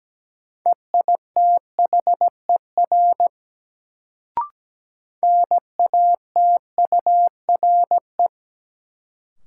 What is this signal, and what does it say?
Morse code, heard as a keyed tone, about 17 wpm, 700 Hz. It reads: EITHER NATURE
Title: Sentences from Top 600 Words - New Word Drill - 17wpm